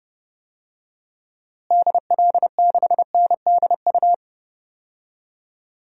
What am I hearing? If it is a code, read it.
Morse code, heard as a keyed tone, about 30 wpm, 700 Hz. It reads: DL6NDU
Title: DXCC-100 Call Signs - 6 Characters - Repeated 2x - 30wpm